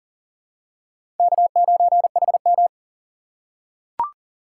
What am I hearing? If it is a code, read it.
Morse code, heard as a keyed tone, about 40 wpm, 700 Hz. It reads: K9HM
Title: USA Call Signs - 4 Characters - 3x Character Spacing - 40wpm